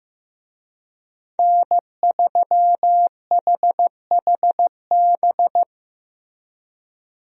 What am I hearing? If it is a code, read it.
Morse code, heard as a keyed tone, about 15 wpm, 700 Hz. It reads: N3HHB
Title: USA Call Signs - 5 Characters - Repeated 2x - 15wpm